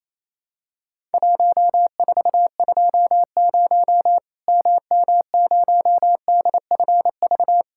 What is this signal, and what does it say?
Morse code, heard as a keyed tone, about 27 wpm, 700 Hz. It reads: 1420 MM0DFV
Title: Worked All Europe DX Contest - QTC Exchange - 28wpm